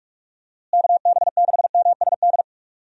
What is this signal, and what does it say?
Morse code, heard as a keyed tone, about 46 wpm, 700 Hz. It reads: KB6MSD
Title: USA Call Signs - 6 Characters - Repeated 3x - 45wpm